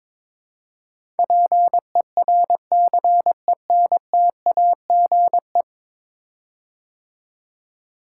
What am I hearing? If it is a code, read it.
Morse code, heard as a keyed tone, about 22 wpm, 700 Hz. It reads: PERCENTAGE
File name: practice / 22wpm / 10 Letter Words - First 3 Letters Given - 22wpm